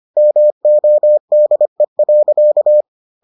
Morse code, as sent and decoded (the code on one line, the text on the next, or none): -- --- -.. .-.-.-
MODE.